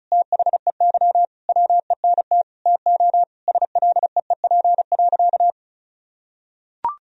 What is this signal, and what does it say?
Morse code, heard as a keyed tone, about 36 wpm, 700 Hz. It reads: THEY WENT TO SLEEP.
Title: Sentences from Top 1000 Words - Review Part 2 - 35wpm